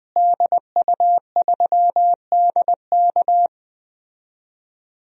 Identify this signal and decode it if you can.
Morse code, heard as a keyed tone, about 20 wpm, 700 Hz. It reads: DU3DK
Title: DXCC-100 Call Signs - 5 Characters - Repeated 3x - 20wpm